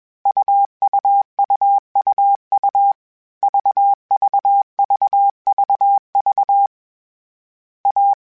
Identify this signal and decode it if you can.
Morse code, heard as a keyed tone, about 21 wpm, 800 Hz. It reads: UUUUU VVVVV A
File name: Difficult Combos / Difficult Combos-mp3s / AUV AUV4 @20